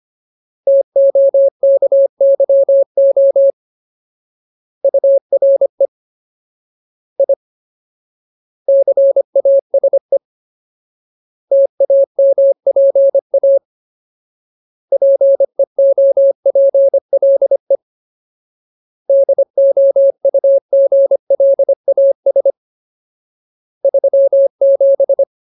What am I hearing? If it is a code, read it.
Morse code, heard as a keyed tone, about 25 wpm, 550 Hz. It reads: TOKYO URE I CASE TAMPA PEOPLE DOUGLAS 37
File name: Ultimate Word List / 25WPM Ultimate